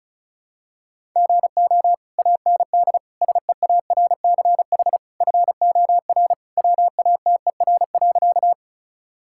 Morse code, heard as a keyed tone, about 36 wpm, 700 Hz. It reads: GO AND SEARCH FOR WATER.
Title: Sentences from Top 1000 Words - Review Part 5 - 35wpm